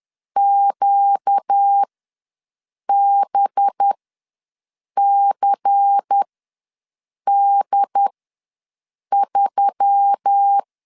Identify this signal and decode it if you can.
Morse code, heard as a keyed tone, about 10 wpm, spaced as for 6 wpm, 800 Hz. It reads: QBCD3
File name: LEZIONI / 10 SIGNS / 10A SLASH ws 10wpm 4121